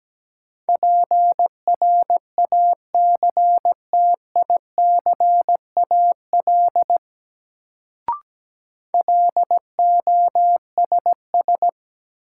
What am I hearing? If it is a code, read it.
Morse code, heard as a keyed tone, about 17 wpm, 700 Hz. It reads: PRACTICAL LOSS
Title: Sentences from Top 1400 Words - New Word Drill - 17wpm